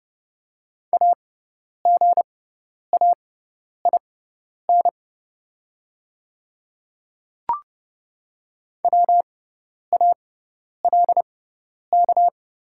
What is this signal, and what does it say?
Morse code, heard as a keyed tone, about 30 wpm, spaced as for 10 wpm, 700 Hz. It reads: AGAIN WALK
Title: 6x Character Spacing - Top 300 Words - 30wpm